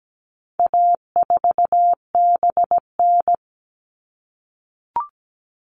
Morse code, heard as a keyed tone, about 17 wpm, 700 Hz. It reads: A4BN
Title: All Cut Numbers - 4-Digit 17wpm